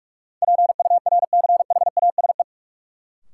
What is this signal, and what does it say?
Morse code, heard as a keyed tone, about 43 wpm, 700 Hz. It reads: PURCHASE
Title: Sentences from Top 800 Words - New Word Drill - 45wpm